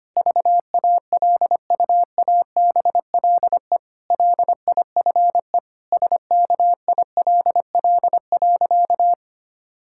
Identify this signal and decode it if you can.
Morse code, heard as a keyed tone, about 25 wpm, 700 Hz. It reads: VALUABLE LIFE SKILL.
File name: practice / 25wpm / Difficult Sentences with Hints - 25wpm